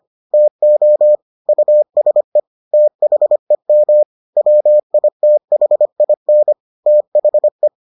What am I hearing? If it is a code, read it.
Morse code, heard as a keyed tone, about 25 wpm, 600 Hz. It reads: TO USE THEM WITHIN THE